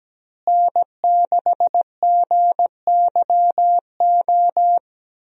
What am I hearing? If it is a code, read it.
Morse code, heard as a keyed tone, about 17 wpm, 700 Hz. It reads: N6GYO
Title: USA Call Signs - 5 Characters - Repeated 3x - 17wpm